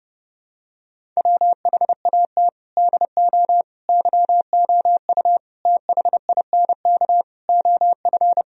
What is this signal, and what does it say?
Morse code, heard as a keyed tone, about 32 wpm, 700 Hz. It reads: WHAT DO YOU THINK OF
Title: Sentences from Top 1000 Words - Review Part 1 - 30wpm